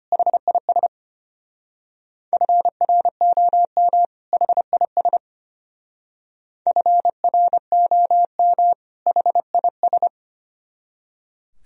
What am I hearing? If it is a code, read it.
Morse code, heard as a keyed tone, about 30 wpm, 700 Hz. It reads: HIS FROM HIS FROM HIS
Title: Speed-Racing - Sets of 2 Words 25wpm